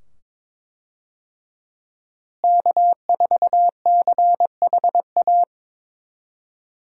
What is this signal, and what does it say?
Morse code, heard as a keyed tone, about 22 wpm, 700 Hz. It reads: K4CHA
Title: USA Call Signs - 5 Characters - Repeated 2x - 22wpm